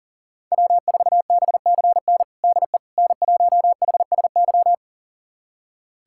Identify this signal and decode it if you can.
Morse code, heard as a keyed tone, about 40 wpm, 700 Hz. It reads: W4BCN DE N1HSY
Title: USA Callsign DE Callsign - 5 Characters - 40wpm